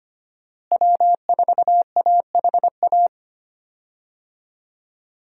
Morse code, heard as a keyed tone, about 25 wpm, 700 Hz. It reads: W4AHA
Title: USA Call Signs - 5 Characters - Repeated 3x - 25wpm